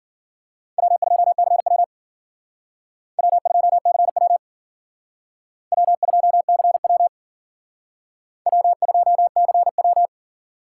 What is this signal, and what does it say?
Morse code, heard as a keyed tone, about 53 wpm, 700 Hz. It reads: W2CW W2CW W2CW W2CW